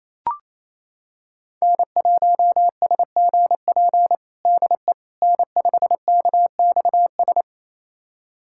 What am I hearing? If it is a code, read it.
Morse code, heard as a keyed tone, about 28 wpm, 700 Hz. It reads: N1SGP DE N5KXS
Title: USA Callsign DE Callsign - 5 Characters - 28wpm